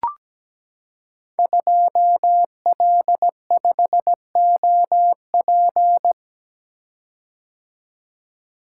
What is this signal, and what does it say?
Morse code, heard as a keyed tone, about 17 wpm, 700 Hz. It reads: 2L5OP
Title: Code Groups - 5 Letter:Numbers - 17wpm